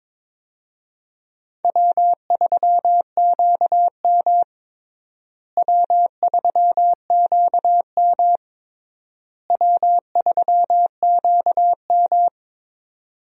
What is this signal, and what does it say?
Morse code, heard as a keyed tone, about 22 wpm, 700 Hz. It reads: W3QM W3QM W3QM
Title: USA Call Signs - 4 Characters - Repeated 3x - 22wpm